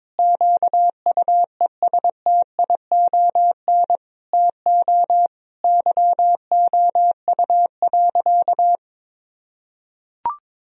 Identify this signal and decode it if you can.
Morse code, heard as a keyed tone, about 22 wpm, 700 Hz. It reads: QUESTION TO YOU.
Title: Sentences from Top 1000 Words - Review Part 2 - 22wpm